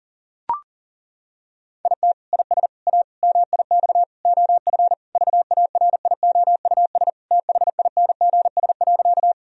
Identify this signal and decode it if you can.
Morse code, heard as a keyed tone, about 40 wpm, 700 Hz. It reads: IT IS A MIX OF VARIOUS THINGS.
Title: Sentences from Top 1000 Words - Review Part 4 - 40wpm